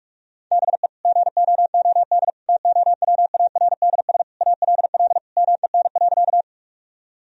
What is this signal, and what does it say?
Morse code, heard as a keyed tone, about 46 wpm, 700 Hz. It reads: BE GOOD TOWARDS ALL MEN.
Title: Sentences from Top 1000 Words - Review Part 2 - 45wpm